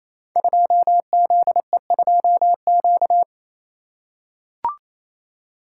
Morse code, heard as a keyed tone, about 27 wpm, 700 Hz. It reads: 2ZE2Q
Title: Code Groups - 5 Letter:Numbers - 28wpm